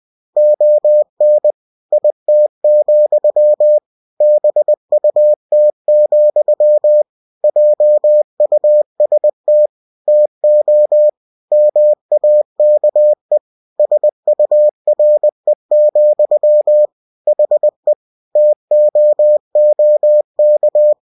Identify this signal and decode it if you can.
Morse code, heard as a keyed tone, about 20 wpm, 600 Hz. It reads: ON IT, BUT, JUST TO MAKE SURE, HE TOOK
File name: cw-books / winnie-the-pooh / winnie-20wpm-page-0037